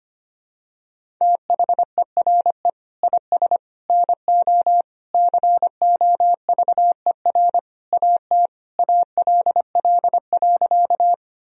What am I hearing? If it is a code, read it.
Morse code, heard as a keyed tone, about 25 wpm, 700 Hz. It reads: THERE IS NO COVER AT ALL.
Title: Sentences from Top 1000 Words - Review Part 1 - 25wpm